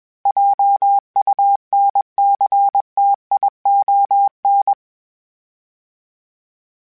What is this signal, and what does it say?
Morse code, heard as a keyed tone, about 21 wpm, 800 Hz. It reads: JUNCTION
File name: fromSMs / FromSMs mp3s / Cities@20